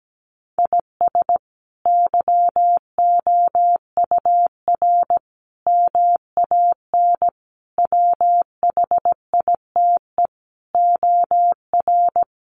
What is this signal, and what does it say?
Morse code, heard as a keyed tone, about 17 wpm, 700 Hz. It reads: IS YOUR MAN WHITE OR
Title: Sentences from Top 400 Words 17wpm